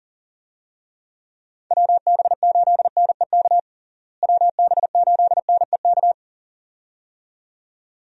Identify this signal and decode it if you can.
Morse code, heard as a keyed tone, about 40 wpm, 700 Hz. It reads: WB8NEK WB8NEK
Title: USA Call Signs - 6 Characters - Repeated 2x - 40wpm